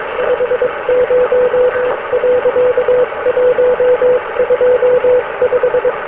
Morse code, heard as a keyed tone, about 22 wpm, spaced as for 19 wpm, 500 Hz, with noise. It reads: 50.125